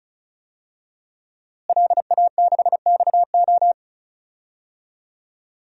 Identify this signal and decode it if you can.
Morse code, heard as a keyed tone, about 36 wpm, 700 Hz. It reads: LA6XO
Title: DXCC-100 Call Signs - 5 Characters - Repeated 3x - 35wpm